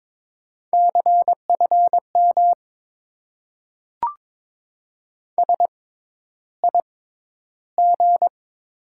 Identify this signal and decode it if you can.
Morse code, heard as a keyed tone, about 22 wpm, spaced as for 11 wpm, 700 Hz. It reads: CFM SIG
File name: Top 90 QSO Elements - 6x Character Spacing - 22wpm